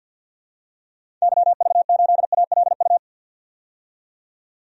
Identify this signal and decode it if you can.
Morse code, heard as a keyed tone, about 52 wpm, 700 Hz. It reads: YV8ALU
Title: DXCC-100 Call Signs - 6 Characters - 50wpm